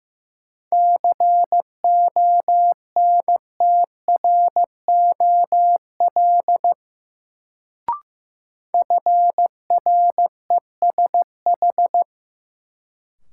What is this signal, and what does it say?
Morse code, heard as a keyed tone, about 15 wpm, 700 Hz. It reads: CONTROL FRESH
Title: Sentences from Top 1200 Words - New Word Drill Alt - 15wpm